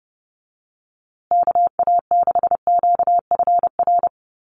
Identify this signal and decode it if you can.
Morse code, heard as a keyed tone, about 30 wpm, 700 Hz. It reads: KA6QFR
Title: USA Call Signs Repeated 3x 30wpm